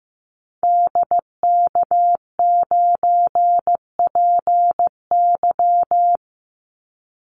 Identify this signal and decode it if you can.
Morse code, heard as a keyed tone, about 15 wpm, 700 Hz. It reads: DK9PY
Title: Call Signs Repeated 2x - 15wpm